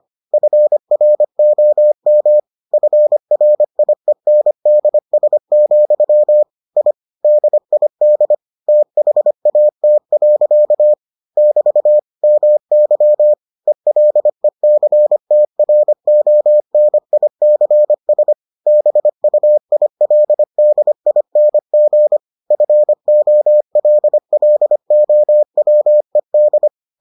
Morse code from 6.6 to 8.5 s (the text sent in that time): I DID